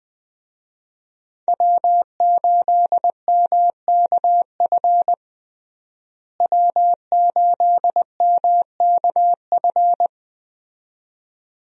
Morse code, heard as a keyed tone, about 20 wpm, 700 Hz. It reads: W8MKF W8MKF
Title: USA Call Signs - 5 Characters - Repeated 2x - 20wpm